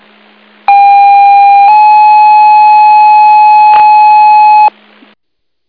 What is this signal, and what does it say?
Two-tone page, heard as a one-second tone, then a three-second tone, about 800 then 850 Hz. It is heard from 0.7 to 4.7 s.